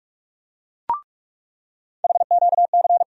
Morse code, heard as a keyed tone, about 41 wpm, 700 Hz. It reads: HQC
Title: USA Call Sign Postfixes - 3 Letters 45wpm